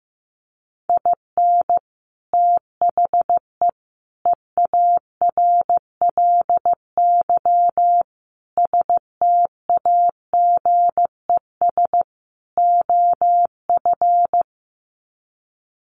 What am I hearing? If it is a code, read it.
Morse code, heard as a keyed tone, about 15 wpm, 700 Hz. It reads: IN THE EARLY STAGES OF
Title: Sets of 5 Words Encore 15wpm